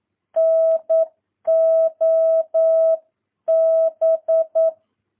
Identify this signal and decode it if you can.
Morse code, heard as a keyed tone, about 9 wpm, 650 Hz. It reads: NOB
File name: jh8cbh / 2022cw